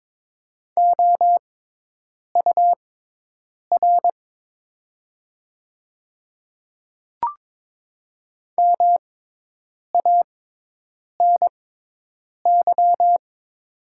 Morse code, heard as a keyed tone, about 22 wpm, spaced as for 8 wpm, 700 Hz. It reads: OUR MANY
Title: 6x Character Spacing - Top 200 Words - 22wpm